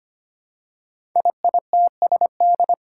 Morse code, heard as a keyed tone, about 25 wpm, 700 Hz. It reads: IITSD
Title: Code Groups - 5 Letters - 25wpm